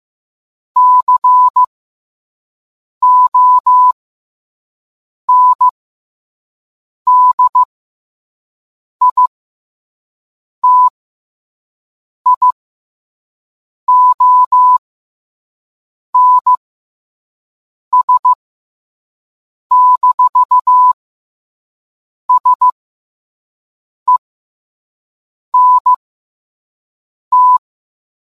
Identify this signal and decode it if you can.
Morse code, heard as a keyed tone, about 15 wpm, spaced as for 5 wpm, 1000 Hz. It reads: CONDITIONS-SENT